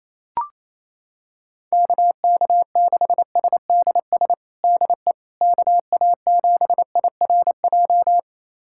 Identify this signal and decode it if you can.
Morse code, heard as a keyed tone, about 28 wpm, 700 Hz. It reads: KK6SDS DE KA7IRJ
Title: USA Callsign DE Callsign - 6 Characters - 28wpm